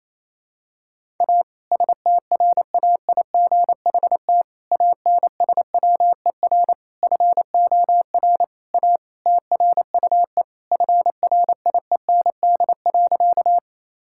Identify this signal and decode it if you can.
Morse code, heard as a keyed tone, about 28 wpm, 700 Hz. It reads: A STRAIGHT ANSWER FOR A TRUE FRIEND.
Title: Sentences from Top 1000 Words - Review Part 6 - 28wpm